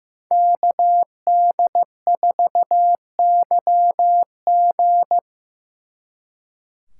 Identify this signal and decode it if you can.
Morse code, heard as a keyed tone, about 18 wpm, 700 Hz. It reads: KD4YG
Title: Speed Racing - USA 2x2 Call Signs 15wpm